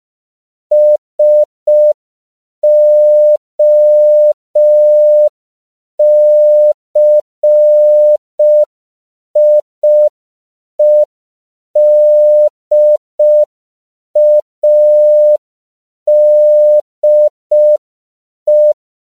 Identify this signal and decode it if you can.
Morse code, heard as a keyed tone, about 5 wpm, 600 Hz. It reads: SOCIEDADE
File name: CW / 5PPM